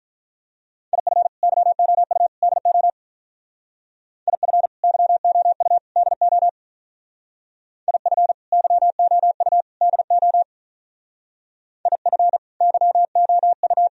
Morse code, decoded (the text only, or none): IF YOU DO IF YOU DO IF YOU DO IF YOU